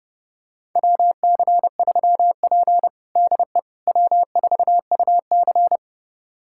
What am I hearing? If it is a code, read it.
Morse code, heard as a keyed tone, about 32 wpm, 700 Hz. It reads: WC3P DE W4UC
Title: USA Callsign DE Callsign - 4 Characters - 30wpm